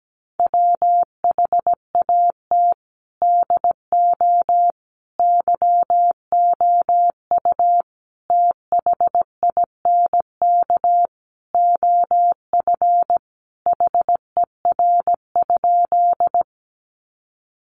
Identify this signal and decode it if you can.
Morse code, heard as a keyed tone, about 17 wpm, 700 Hz. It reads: WHAT DO YOU THINK OF HER?
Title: Sentences from Top 200 Words 17wpm